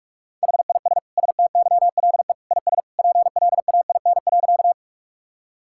Morse code, heard as a keyed tone, about 46 wpm, 700 Hz. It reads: HIS STYLE IS PLAIN.
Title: Sentences from Top 1000 Words - Review Part 5 - 45wpm